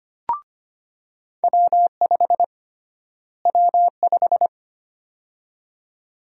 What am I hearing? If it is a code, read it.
Morse code, heard as a keyed tone, about 25 wpm, 700 Hz. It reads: W5 W5
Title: USA Call Sign Prefixes - Repeated 2x - 25wpm